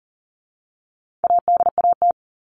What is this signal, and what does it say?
Morse code, heard as a keyed tone, about 40 wpm, 700 Hz. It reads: ADAT
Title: All Cut Numbers - 4-Digit 40wpm